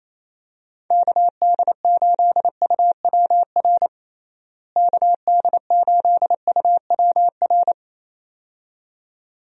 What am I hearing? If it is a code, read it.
Morse code, heard as a keyed tone, about 27 wpm, 700 Hz. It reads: KD8UWR KD8UWR
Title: USA Call Signs - 6 Characters - Repeated 2x - 28wpm